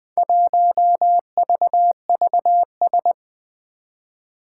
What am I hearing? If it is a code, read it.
Morse code, heard as a keyed tone, about 20 wpm, 700 Hz. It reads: 1VVS
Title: Four Challenging Letter-Number Combinations 20wpm